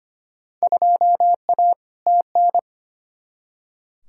Morse code, heard as a keyed tone, about 25 wpm, 700 Hz. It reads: 2A TN
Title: Field Day Exchanges - 25wpm